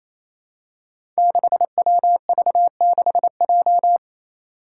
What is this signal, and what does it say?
Morse code, heard as a keyed tone, about 28 wpm, 700 Hz. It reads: 6WV6J
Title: Five Challenging Letter-Number Combinations 28wpm